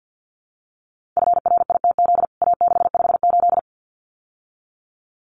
Morse code, heard as a keyed tone, about 50 wpm, 700 Hz. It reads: FRITZ A658
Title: CWT Exchange 50wpm